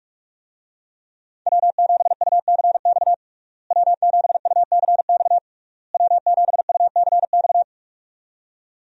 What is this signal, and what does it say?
Morse code, heard as a keyed tone, about 41 wpm, 700 Hz. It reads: W7UCX W7UCX W7UCX
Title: USA Call Signs - 5 Characters - Repeated 3x - 45wpm